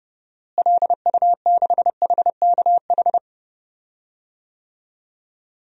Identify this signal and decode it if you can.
Morse code, heard as a keyed tone, about 31 wpm, 700 Hz. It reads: LU6HKH
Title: DXCC-100 Call Signs - 6 Characters - Repeated 3x - 30wpm